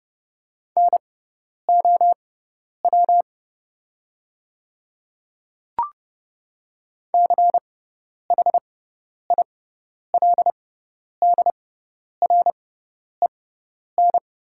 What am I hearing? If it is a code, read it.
Morse code, heard as a keyed tone, about 30 wpm, spaced as for 10 wpm, 700 Hz. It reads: NOW CHILDREN
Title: 6x Character Spacing - Top 300 Words - 30wpm